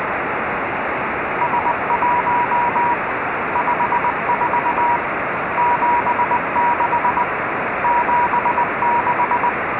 Morse code, heard as a keyed tone, about 20 wpm, 960 Hz, with radio noise. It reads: S1 54 76 76